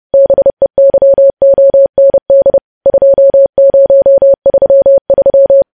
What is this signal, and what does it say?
Morse code, heard as a keyed tone, about 30 wpm, 550 Hz. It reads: BEYOND 2033
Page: news headlines in morse code at 30 wpm